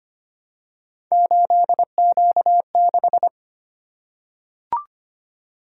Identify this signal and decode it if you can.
Morse code, heard as a keyed tone, about 25 wpm, 700 Hz. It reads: 8Q6
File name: Three Letter-Number Combinations - 25wpm